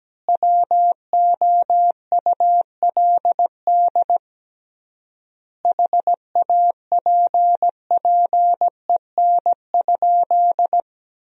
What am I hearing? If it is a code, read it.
Morse code, heard as a keyed tone, about 17 wpm, 700 Hz. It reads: WOULD HAPPEN?
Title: 3x Word-Spacing - Sentences from Top 400 Words - 17wpm